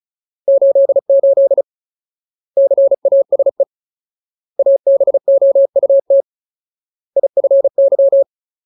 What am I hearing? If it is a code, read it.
Morse code, heard as a keyed tone, about 34 wpm, 550 Hz. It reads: 88 CASE ABOUT IFY
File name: Ultimate Word List / 35WPM Ultimate